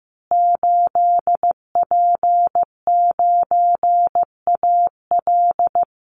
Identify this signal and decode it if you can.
Morse code, heard as a keyed tone, about 15 wpm, 700 Hz. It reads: 8P9AL
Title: Call Signs Repeated 2x - 15wpm